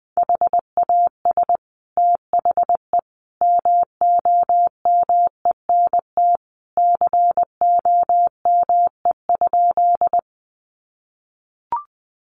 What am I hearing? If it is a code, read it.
Morse code, heard as a keyed tone, about 20 wpm, 700 Hz. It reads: HAS THE MOMENT COME?
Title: Sentences from Top 1000 Words 20wpm